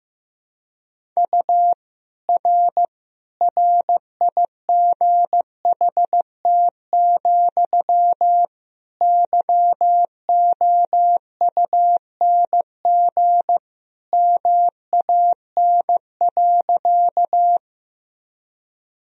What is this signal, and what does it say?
Morse code, heard as a keyed tone, about 15 wpm, 700 Hz. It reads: U R RIGHT, YOUNG MAN.
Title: Sentences from Top 400 Words and Abbreviations 15wpm